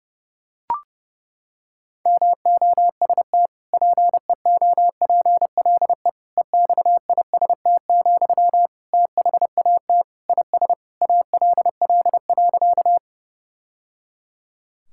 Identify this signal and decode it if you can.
Morse code, heard as a keyed tone, about 32 wpm, 700 Hz. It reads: MOST PEOPLE EXIST, THAT IS ALL.
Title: Sentences from Top 1000 Words - Review Part 5 - 30wpm